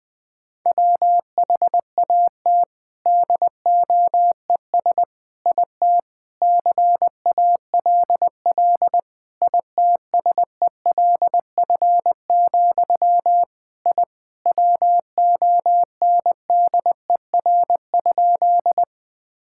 Morse code, heard as a keyed tone, about 20 wpm, 700 Hz. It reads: WHAT DOES IT CALL ITSELF, I WONDER?
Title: Sentences from Top 1000 Words - Review Part 5 - 20wpm